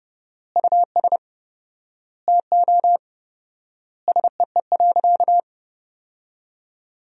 Morse code, heard as a keyed tone, about 32 wpm, 700 Hz. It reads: US TO SEE.
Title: 4x Word-Spacing - Sentences from Top 500 Words - 30wpm